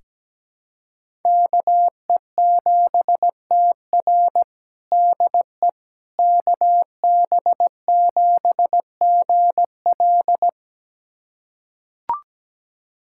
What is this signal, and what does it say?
Morse code, heard as a keyed tone, about 17 wpm, 700 Hz. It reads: KE7TR DE KB7GL
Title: USA Callsign DE Callsign - 5 Characters - 17wpm